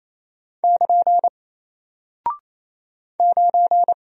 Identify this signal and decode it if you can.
Morse code, heard as a keyed tone, about 27 wpm, 700 Hz. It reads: <KN> 9
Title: Single Character (Letter:Number:Punctuation:Prosign) Alt 28wpm